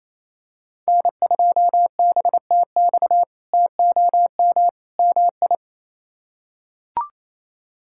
Morse code, heard as a keyed tone, about 28 wpm, 700 Hz. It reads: N2BTX TOM MI
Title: K1USN SST QSO Practice - 28wpm